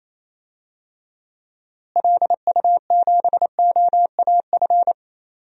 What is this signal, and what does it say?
Morse code, heard as a keyed tone, about 28 wpm, 700 Hz. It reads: LU7OAF
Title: DXCC-100 Call Signs - 6 Characters - Repeated 2x - 28wpm